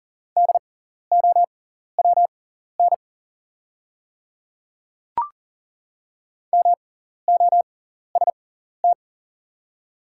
Morse code, heard as a keyed tone, about 40 wpm, spaced as for 14 wpm, 690 Hz. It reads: DOWN MOST
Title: 6x Character Spacing - Top 100 Words - 40wpm